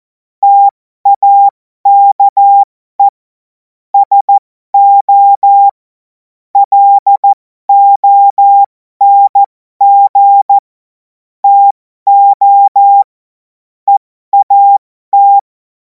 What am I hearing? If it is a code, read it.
Morse code, heard as a keyed tone, about 14 wpm, spaced as for 12 wpm, 800 Hz. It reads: TAKE SO LONG TO EAT
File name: Morsekurs / Witze